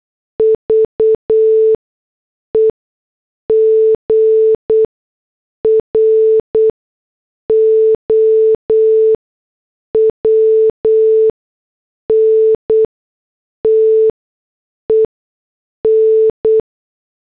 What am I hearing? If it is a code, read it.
Morse code, heard as a keyed tone, about 8 wpm, spaced as for 6 wpm, 440 Hz. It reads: VE GROWN TEN
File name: morse / headlines